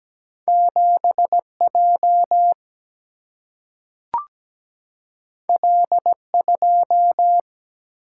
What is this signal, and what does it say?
Morse code, heard as a keyed tone, about 17 wpm, 700 Hz. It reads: 7 J L 2
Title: Single-Letter Single-Number 17wpm